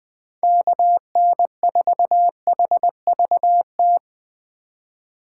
Morse code, heard as a keyed tone, about 21 wpm, 700 Hz. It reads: KN4HVT